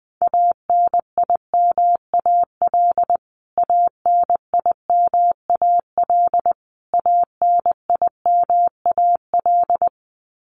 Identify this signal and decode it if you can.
Morse code, heard as a keyed tone, about 20 wpm, 700 Hz. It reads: ANIMAL ANIMAL ANIMAL
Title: Top 200 Words Repeated 20wpm